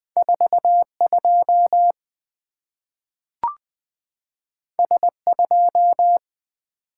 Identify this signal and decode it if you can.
Morse code, heard as a keyed tone, about 20 wpm, 700 Hz. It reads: 42 S2
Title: Two Letter-Number Combinations - 20wpm